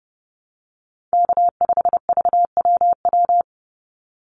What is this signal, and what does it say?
Morse code, heard as a keyed tone, about 30 wpm, 700 Hz. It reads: K5VWW
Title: Call Signs Repeated 2x - 30wpm